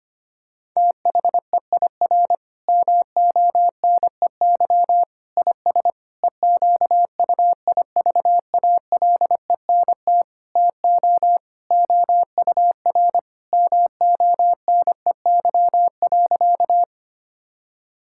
Morse code, heard as a keyed tone, about 25 wpm, 700 Hz. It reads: THEIR MONEY IS EQUIVALENT TO OUR MONEY.
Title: Sentences from Top 1600 Words 25wpm